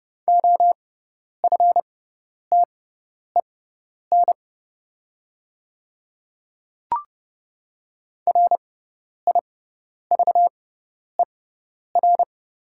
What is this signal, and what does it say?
Morse code, heard as a keyed tone, about 30 wpm, spaced as for 10 wpm, 700 Hz. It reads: OFTEN RIVER
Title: 6x Character Spacing - Top 300 Words - 30wpm